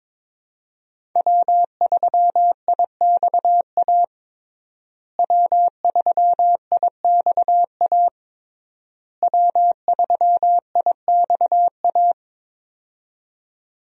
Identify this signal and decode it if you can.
Morse code, heard as a keyed tone, about 22 wpm, 700 Hz. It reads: W3IXA W3IXA W3IXA
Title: USA Call Signs - 5 Characters - Repeated 3x - 22wpm